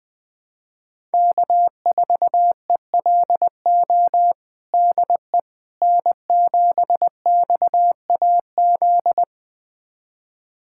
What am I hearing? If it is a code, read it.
Morse code, heard as a keyed tone, about 20 wpm, 700 Hz. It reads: K4ELO DE N7XAZ
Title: USA Callsign DE Callsign 20wpm